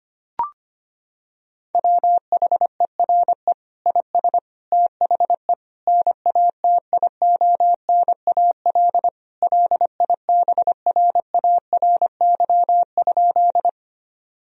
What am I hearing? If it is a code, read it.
Morse code, heard as a keyed tone, about 25 wpm, 700 Hz. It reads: WHERE IS THE NATIONAL LIBRARY?
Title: Sentences from Top 1000 Words - Review Part 4 - 25wpm